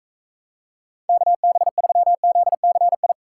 Morse code, heard as a keyed tone, about 45 wpm, 700 Hz. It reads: KB3ZCI